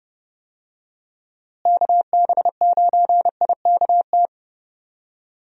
Morse code, heard as a keyed tone, about 30 wpm, 700 Hz. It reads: KB9IKT
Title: USA Call Signs - 6 Characters - Repeated 3x - 30wpm